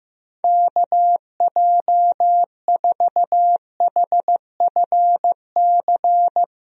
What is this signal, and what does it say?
Morse code, heard as a keyed tone, about 15 wpm, 700 Hz. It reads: KJ4HFC
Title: USA Call Signs - 6 Characters - Repeated 3x - 15wpm